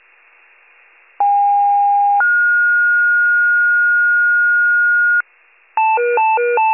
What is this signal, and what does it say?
Two-tone page, heard as a one-second tone, then a three-second tone, about 810 then 1430 Hz, with noise. It is heard from 1.2 to 5.2 s.